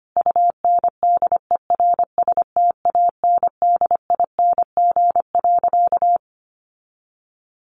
Morse code, heard as a keyed tone, about 25 wpm, 700 Hz. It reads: UNDERSTANDING.
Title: Sentences from Top 1400 Words 25wpm